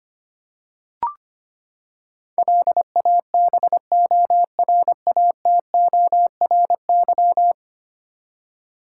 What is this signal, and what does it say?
Morse code, heard as a keyed tone, about 25 wpm, 700 Hz. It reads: LABORATORY
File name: Ten Letter Words - 25wpm